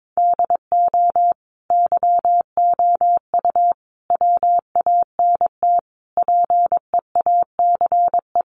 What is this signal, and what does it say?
Morse code, heard as a keyed tone, about 22 wpm, 700 Hz. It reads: DO YOU WANT PEACE
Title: Sentences from Top 1500 Words 22wpm